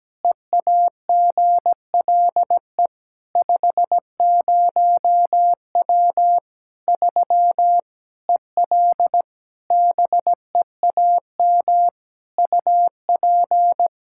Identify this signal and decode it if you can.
Morse code, heard as a keyed tone, about 17 wpm, 700 Hz. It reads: EAGLE 50W 3 EL BEAM UP
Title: Practice QSO - Exchange 2 - Info - 17wpm